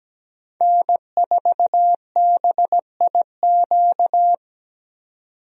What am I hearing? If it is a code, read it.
Morse code, heard as a keyed tone, about 17 wpm, 700 Hz. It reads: N4BIQ
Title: USA Call Signs - 5 Characters - 17wpm